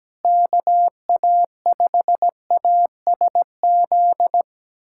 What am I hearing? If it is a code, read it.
Morse code, heard as a keyed tone, about 17 wpm, 700 Hz. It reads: KA5ASZ
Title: USA Callsign DE Callsign - 6 Characters - 17wpm